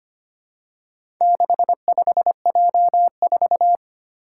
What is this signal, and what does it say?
Morse code, heard as a keyed tone, about 25 wpm, 700 Hz. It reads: 65J4
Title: Four Challenging Letter-Number Combinations 25wpm